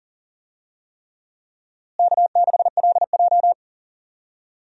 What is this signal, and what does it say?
Morse code, heard as a keyed tone, about 40 wpm, 700 Hz. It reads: K6LJ
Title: USA Call Signs - 4 Characters - Repeated 3x - 40wpm